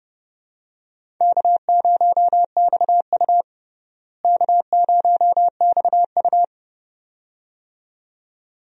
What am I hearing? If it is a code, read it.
Morse code, heard as a keyed tone, about 30 wpm, 700 Hz. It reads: K0XU K0XU
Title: USA Call Signs - 4 Characters - Repeated 2x - 30wpm